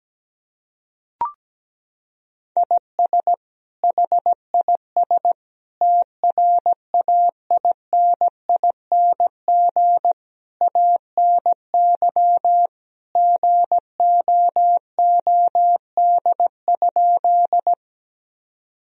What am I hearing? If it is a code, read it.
Morse code, heard as a keyed tone, about 17 wpm, 700 Hz. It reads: IS HIS TRAINING ANY GOOD?
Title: Sentences from Top 1000 Words - Review Part 4 - 17wpm